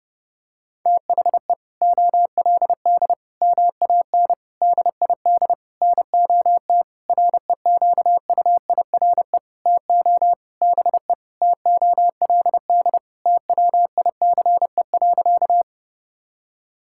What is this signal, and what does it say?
Morse code, heard as a keyed tone, about 30 wpm, 700 Hz. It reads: THE OLD MAN DID NOT REQUIRE TO BE TOLD TWICE.
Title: Sentences from Top 1600 Words 30wpm